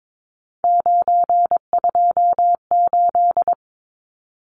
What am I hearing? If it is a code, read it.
Morse code, heard as a keyed tone, about 22 wpm, 700 Hz. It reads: 928
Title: Numbers - 3-Digit 22wpm